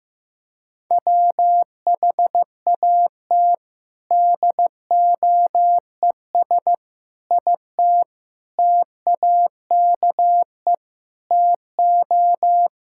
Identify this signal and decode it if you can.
Morse code, heard as a keyed tone, about 15 wpm, 700 Hz. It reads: WHAT DOES IT TAKE TO
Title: Sentences from Top 1600 Words 15wpm